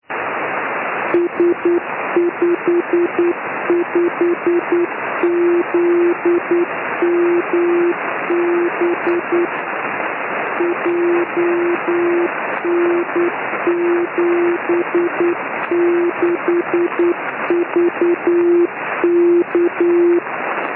Morse code, heard as a keyed tone, about 9 wpm, 350 Hz, with noise. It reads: S55ZMB JN76VK
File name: ei4dq / s55zmb beacon